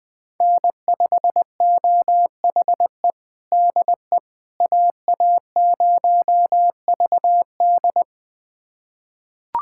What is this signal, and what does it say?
Morse code, heard as a keyed tone, about 21 wpm, 700 Hz. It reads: N5OHE DE AA0VD
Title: USA Callsign DE Callsign - 5 Characters - 20wpm